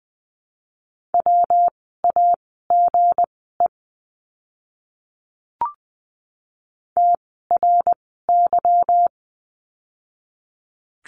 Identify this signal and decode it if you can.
Morse code, heard as a keyed tone, about 19 wpm, spaced as for 14 wpm, 700 Hz. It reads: WAGE TRY
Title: 160 Lesson 26f - Letter g - Words 20wpm